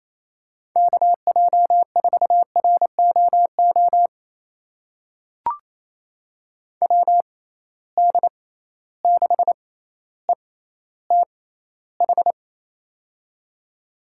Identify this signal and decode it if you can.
Morse code, heard as a keyed tone, about 27 wpm, spaced as for 14 wpm, 700 Hz. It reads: KJ4ROO WD6ETH
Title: USA Call Signs - 6 Characters - 6x Character Spacing - 28wpm